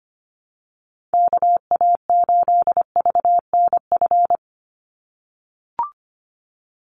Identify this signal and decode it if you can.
Morse code, heard as a keyed tone, about 25 wpm, 700 Hz. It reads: KA8VNF
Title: USA Call Signs 25wpm